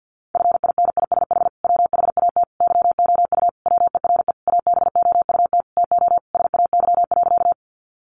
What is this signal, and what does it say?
Morse code, heard as a keyed tone, about 50 wpm, 700 Hz. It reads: FINISH WHAT YOU WERE ABOUT TO SAY.
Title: Sentences from Top 1200 Words 50wpm